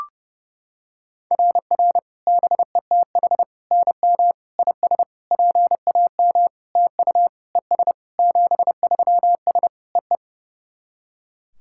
Morse code, heard as a keyed tone, about 30 wpm, 700 Hz. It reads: RR BETH NM IS PAM TU ES 73S EE
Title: SOTA QSO Practice 30wpm